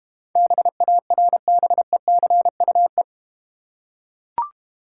Morse code, heard as a keyed tone, about 29 wpm, 700 Hz. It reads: BARBECUE
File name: Easy Words - Speed-Racing with Hints - 25wpm